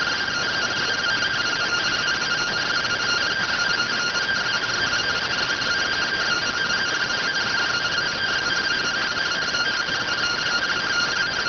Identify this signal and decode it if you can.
Two-tone teleprinter signal, about 170 Hz shift.